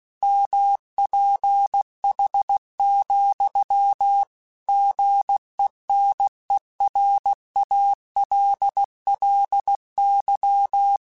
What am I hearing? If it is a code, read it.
Morse code, heard as a keyed tone, about 16 wpm, 800 Hz. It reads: MPH, GENERALLY
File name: Weather Sets / Weather Sets-mp3s / Weather 4 21 @15